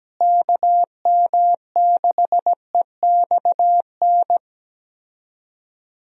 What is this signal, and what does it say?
Morse code, heard as a keyed tone, about 17 wpm, 700 Hz. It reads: KM6EXN
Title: USA Call Signs - 6 Characters - Repeated 2x - 17wpm